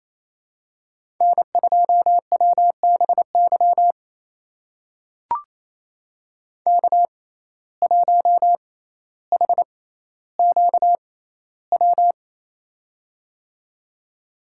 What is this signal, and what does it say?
Morse code, heard as a keyed tone, about 28 wpm, spaced as for 14 wpm, 700 Hz. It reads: N2WBY K1HQW
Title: USA Call Signs - 5 Characters - 6x Character Spacing - 28wpm